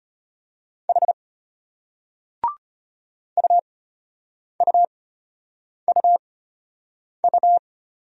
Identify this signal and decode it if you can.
Morse code, heard as a keyed tone, about 34 wpm, spaced as for 8 wpm, 700 Hz. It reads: H UUUU